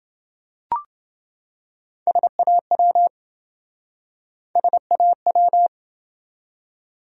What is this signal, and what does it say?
Morse code, heard as a keyed tone, about 27 wpm, 700 Hz. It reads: SAW SAW